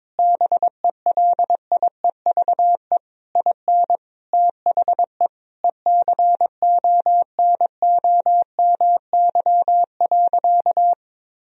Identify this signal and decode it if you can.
Morse code, heard as a keyed tone, about 22 wpm, 700 Hz. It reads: BELIEVE IN THE ECONOMY.
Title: Sentences from Top 1000 Words - Review Part 3 - 22wpm